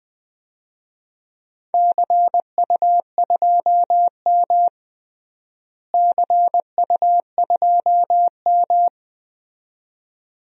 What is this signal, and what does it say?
Morse code, heard as a keyed tone, about 20 wpm, 700 Hz. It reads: CU2M CU2M
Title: DXCC-100 Call Signs - 4 Characters - Repeated 2x - 20wpm